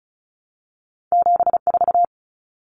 Morse code, heard as a keyed tone, about 36 wpm, 700 Hz. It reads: 74
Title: Common Cut Numbers - 2-Digit 35wpm